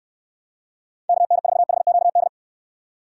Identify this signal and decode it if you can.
Morse code, heard as a keyed tone, about 69 wpm, 700 Hz. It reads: KN4HQD